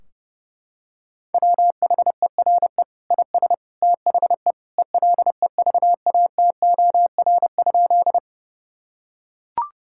Morse code, heard as a keyed tone, about 30 wpm, 700 Hz. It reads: WHERE IS THE ELEVATOR?
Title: Sentences from Top 1600 Words 30wpm